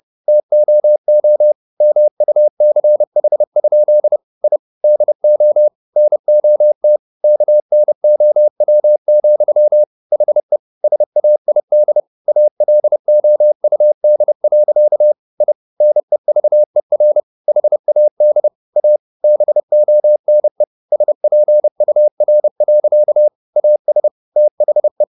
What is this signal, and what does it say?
Morse code, heard as a keyed tone, about 30 wpm, 600 Hz. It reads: TOO MUCH? I DO NOT KNOW, HE SAID ALOUD. I NEVER HAD A BONE SPUR. AS THE
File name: the old man and the sea / the old man and the sea 1 30wpm